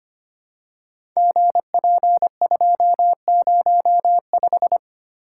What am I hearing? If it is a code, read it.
Morse code, heard as a keyed tone, about 25 wpm, 700 Hz. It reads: GP205
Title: Code Groups - 5 Letter:Numbers - 25wpm